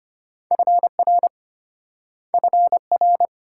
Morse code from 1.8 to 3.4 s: FR

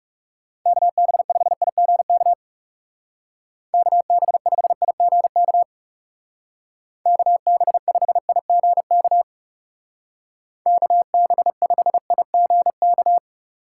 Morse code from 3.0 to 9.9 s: KB5IGK KB5IGK

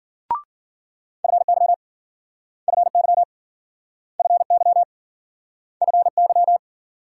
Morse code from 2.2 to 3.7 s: FY

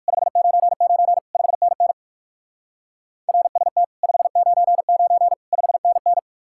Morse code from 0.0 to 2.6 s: 599 5NN

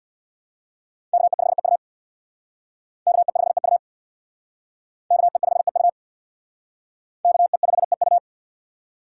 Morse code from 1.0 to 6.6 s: KE5EU KE5EU KE5EU

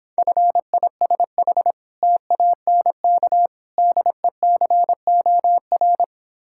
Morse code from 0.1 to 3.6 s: FISH TANK

